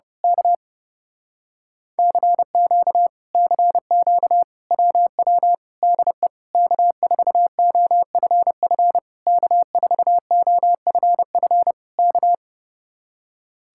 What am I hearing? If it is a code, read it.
Morse code, heard as a keyed tone, about 30 wpm, 700 Hz. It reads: K CQ CQ WW DE K4OFF K4OFF K
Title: Speed-Racing Common QSO Phrases Encore - 30wpm